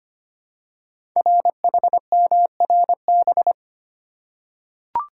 Morse code, heard as a keyed tone, about 25 wpm, 700 Hz. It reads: RHMRB
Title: Code Groups - 5 Letters - 25wpm